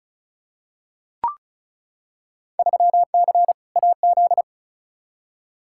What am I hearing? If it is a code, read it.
Morse code, heard as a keyed tone, about 34 wpm, 700 Hz. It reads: 3C AZ
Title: Field Day Exchanges Encore - 35wpm